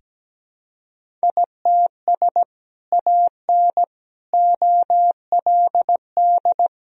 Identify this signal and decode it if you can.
Morse code, heard as a keyed tone, about 17 wpm, 700 Hz. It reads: ITS AN OLD
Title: Sentences from Top 1000 Words - Review Part 2 - 17wpm